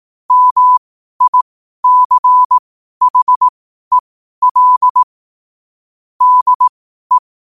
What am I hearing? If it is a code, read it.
Morse code, heard as a keyed tone, about 18 wpm, spaced as for 12 wpm, 1000 Hz. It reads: MICHEL DE
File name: qotdfiles / 12wpm